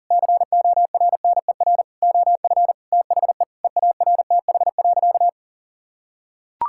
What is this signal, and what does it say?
Morse code, heard as a keyed tone, about 37 wpm, 700 Hz. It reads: CORNER OF THE EARTH.
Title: Sentences from Top 1600 Words 40wpm